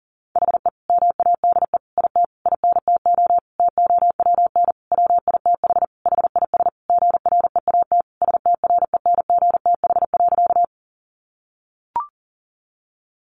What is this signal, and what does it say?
Morse code, heard as a keyed tone, about 40 wpm, 700 Hz. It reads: HE MADE IT INTO TOWN WITH HIS GREAT STRENGTH.
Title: Sentences from Top 1300 Words 40wpm